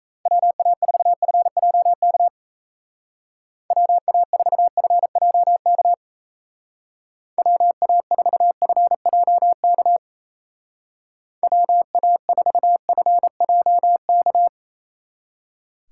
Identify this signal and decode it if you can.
Morse code, heard as a keyed tone, about 33 wpm, 700 Hz. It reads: WA4FJK WA4FJK WA4FJK WA4FJK
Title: Speed Racing - USA 2x3 Call Signs 28wpm